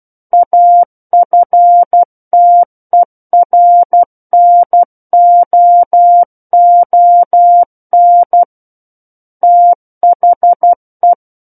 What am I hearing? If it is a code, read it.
Morse code, heard as a keyed tone, about 12 wpm, 700 Hz. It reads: AFTERNOON THE
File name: Book One / 03-Chapter 3 - 12 WPM